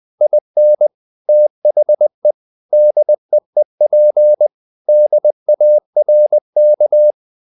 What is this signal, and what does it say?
Morse code, heard as a keyed tone, about 20 wpm, 600 Hz. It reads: IN THE DEEP DARK
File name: the old man and the sea / the old man and the sea part 1 20wpm